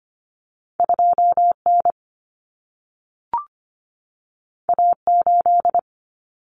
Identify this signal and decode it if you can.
Morse code, heard as a keyed tone, about 25 wpm, 700 Hz. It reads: 2N A8
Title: Common Cut Numbers - 2-Digit 25wpm